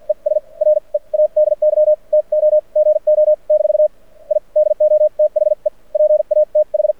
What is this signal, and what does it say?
Morse code, heard as a keyed tone, about 57 wpm, spaced as for 43 wpm, 600 Hz. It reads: ES READY TO GO - I DO THE PATH
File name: archived / CRSnet